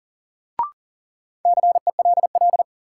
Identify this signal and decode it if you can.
Morse code, heard as a keyed tone, about 40 wpm, 700 Hz. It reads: CELL
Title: Sentences from Top 1100 Words - New Word Drill Alt - 40wpm